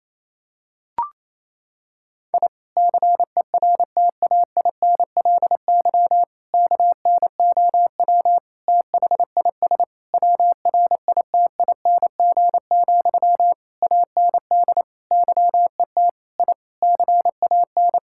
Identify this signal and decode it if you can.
Morse code, heard as a keyed tone, about 28 wpm, 700 Hz. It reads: I CERTAINLY KNOW THIS WRITING, AND YET I CAN
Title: Sentences from Top 1000 Words - Review Part 6 - 28wpm